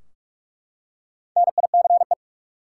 Morse code, heard as a keyed tone, about 43 wpm, 700 Hz. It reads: NICE
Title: Four Letter Words - 45wpm